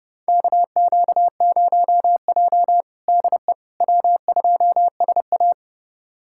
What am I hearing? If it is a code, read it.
Morse code, heard as a keyed tone, about 32 wpm, 700 Hz. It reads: KQ0J DE W2SA
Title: USA Callsign DE Callsign - 4 Characters - 30wpm